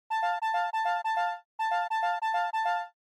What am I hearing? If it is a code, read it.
There is no code.